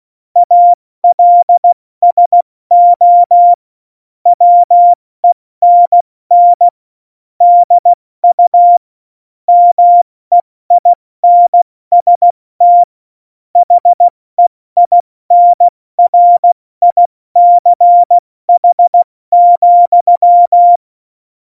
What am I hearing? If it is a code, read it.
Morse code, heard as a keyed tone, about 16 wpm, spaced as for 14 wpm, 700 Hz. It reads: ALSO WENN DU MEINST HEINRICH,
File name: Morsekurs / Witze / Witz 27. August 2019, 70 F80